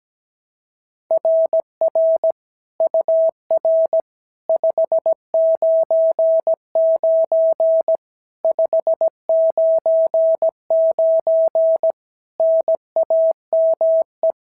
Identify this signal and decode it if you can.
Morse code, heard as a keyed tone, about 16 wpm, 650 Hz. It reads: RR UR 599 599 NAME
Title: SOTA QSO Practice 17wpm